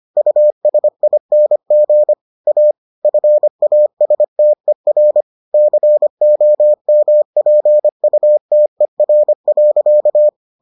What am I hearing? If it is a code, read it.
Morse code, heard as a keyed tone, about 25 wpm, 600 Hz. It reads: USING A FASTER COMPUTER.